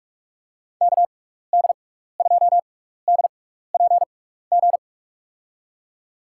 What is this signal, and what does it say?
Morse code, heard as a keyed tone, about 46 wpm, spaced as for 15 wpm, 700 Hz. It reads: KD2DPG
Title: USA Call Signs - 6 Characters - 6x Character Spacing - 45wpm